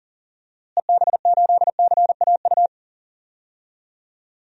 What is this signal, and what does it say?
Morse code, heard as a keyed tone, about 40 wpm, 700 Hz. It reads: EB8CAU
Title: DXCC-100 Call Signs - 6 Characters - Repeated 3x - 40wpm